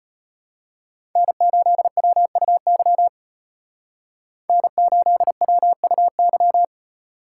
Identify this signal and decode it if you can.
Morse code, heard as a keyed tone, about 32 wpm, 700 Hz. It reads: N8WUY N8WUY